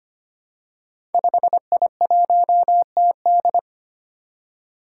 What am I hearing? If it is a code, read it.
Morse code, heard as a keyed tone, about 25 wpm, 700 Hz. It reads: 5I1TD
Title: Five Challenging Letter-Number Combinations 25wpm